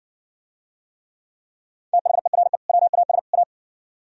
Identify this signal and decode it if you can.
Morse code, heard as a keyed tone, about 57 wpm, 700 Hz. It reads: THERE WAS A